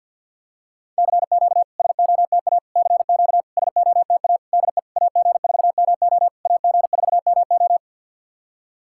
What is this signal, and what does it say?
Morse code, heard as a keyed tone, about 52 wpm, 690 Hz. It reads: CQ SOTA CQ SOTA DE AG4MO AG4MO
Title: SOTA QSO Practice 50wpm